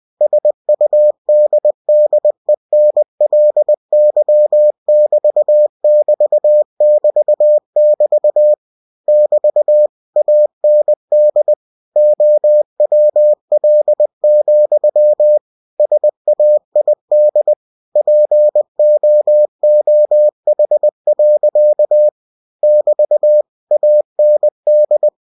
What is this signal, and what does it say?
Morse code, heard as a keyed tone, about 20 wpm, 600 Hz. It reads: SUDDENLY<BT><BT><BT><BT> <BT> AND OWL, SAID POOH. <BT> AND